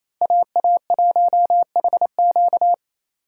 Morse code, heard as a keyed tone, about 28 wpm, 700 Hz. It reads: AA1HQ
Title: USA Call Signs - 5 Characters - 4x Character Spacing - 28wpm